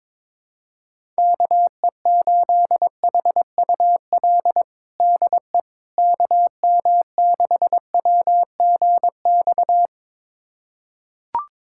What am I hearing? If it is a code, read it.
Morse code, heard as a keyed tone, about 22 wpm, 700 Hz. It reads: KE8HUL DE KM6WGX
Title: USA Callsign DE Callsign 22wpm